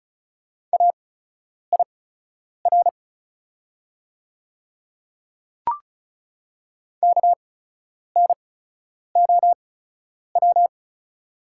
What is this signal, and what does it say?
Morse code, heard as a keyed tone, about 36 wpm, spaced as for 10 wpm, 690 Hz. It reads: AIR KNOW
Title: 8x Character Spacing - Top 200 Words - 35wpm